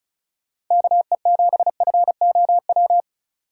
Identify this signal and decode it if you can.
Morse code, heard as a keyed tone, about 46 wpm, 700 Hz. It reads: KE7FOW